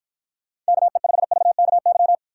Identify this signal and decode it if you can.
Morse code, heard as a keyed tone, about 52 wpm, 700 Hz. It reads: KE5VKY